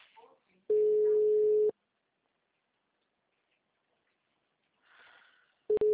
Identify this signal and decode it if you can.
Ringback tone (Europe, 425 Hz), with speech.
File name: out- -2034- - - .77090